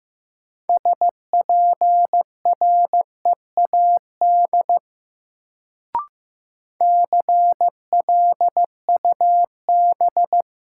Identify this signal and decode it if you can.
Morse code, heard as a keyed tone, about 15 wpm, 700 Hz. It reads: SPREAD CLUB
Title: Sentences from Top 1200 Words - New Word Drill - 15wpm